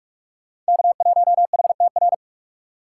Morse code, heard as a keyed tone, about 39 wpm, 700 Hz. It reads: K1HTR